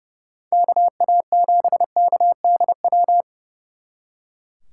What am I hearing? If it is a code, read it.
Morse code, heard as a keyed tone, about 38 wpm, 700 Hz. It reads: KA7KDW